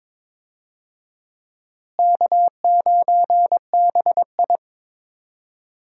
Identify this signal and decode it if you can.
Morse code, heard as a keyed tone, about 22 wpm, 700 Hz. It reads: K9BI
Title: USA Call Signs - 4 Characters - Repeated 2x - 22wpm